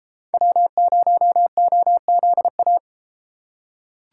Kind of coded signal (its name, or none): Morse code